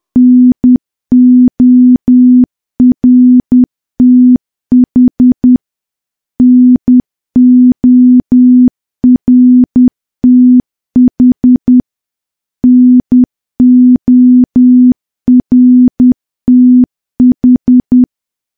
Morse code, heard as a keyed tone, about 10 wpm, 250 Hz. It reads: NORTH NORTH NORTH
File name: morse / morse-training-10WPM